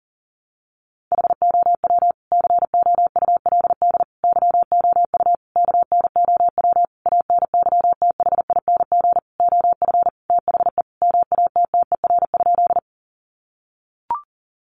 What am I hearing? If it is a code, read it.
Morse code, heard as a keyed tone, about 40 wpm, 700 Hz. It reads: HOW COULD YOU KNOW ANYTHING OF THE MATTER?
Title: Sentences from Top 1000 Words 40wpm